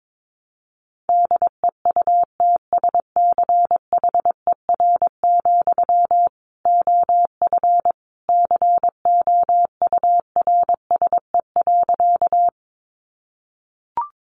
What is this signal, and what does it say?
Morse code, heard as a keyed tone, about 22 wpm, 700 Hz. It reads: DEUTSCHER, OF COURSE.
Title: A Sound Of Thunder 22wpm